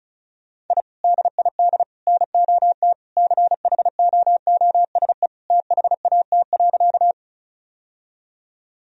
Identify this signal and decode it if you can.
Morse code, heard as a keyed tone, about 36 wpm, 700 Hz. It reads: I DID NOT CHOOSE THAT.
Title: Sentences from Top 1000 Words - Review Part 4 - 35wpm